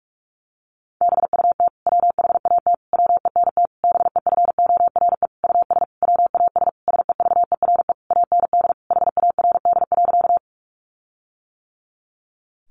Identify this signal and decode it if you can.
Morse code, heard as a keyed tone, about 44 wpm, 700 Hz. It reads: BUT WHAT WENT BEFORE US WAS SEVERE AND HARD.
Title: Sentences from Top 1400 Words 45wpm